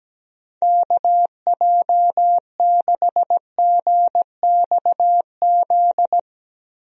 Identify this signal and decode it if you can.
Morse code, heard as a keyed tone, about 17 wpm, 700 Hz. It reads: KJ6GXZ
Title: USA Call Signs - 6 Characters - 17wpm